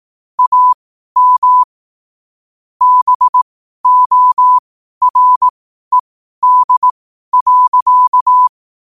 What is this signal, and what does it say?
Morse code, heard as a keyed tone, about 18 wpm, spaced as for 12 wpm, 1000 Hz. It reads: AM BORED.
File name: qotdfiles / 12wpm